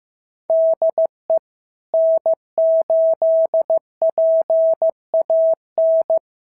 Morse code, heard as a keyed tone, about 14 wpm, 650 Hz. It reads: DE N8PAN
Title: SOTA QSO Practice 15wpm